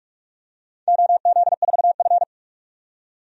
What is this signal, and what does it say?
Morse code, heard as a keyed tone, about 42 wpm, 700 Hz. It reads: OZ4F
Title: DXCC-100 Call Signs - 4 Characters - 45wpm